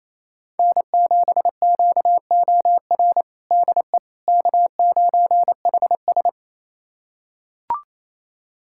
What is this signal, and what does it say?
Morse code, heard as a keyed tone, about 28 wpm, 700 Hz. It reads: N7QOR DE K9HS
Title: USA Callsign DE Callsign 28wpm